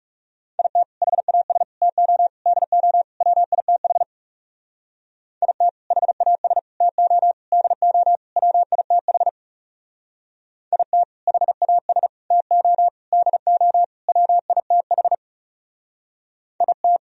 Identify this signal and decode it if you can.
Morse code, heard as a keyed tone, about 38 wpm, 700 Hz. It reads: IT HAS TO DO WITH IT HAS TO DO WITH IT HAS TO DO WITH IT